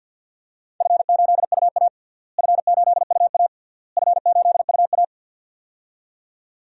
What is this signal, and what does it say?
Morse code, heard as a keyed tone, about 47 wpm, 700 Hz. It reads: F8UA F8UA F8UA
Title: DXCC-100 Call Signs - 4 Characters - Repeated 3x - 50wpm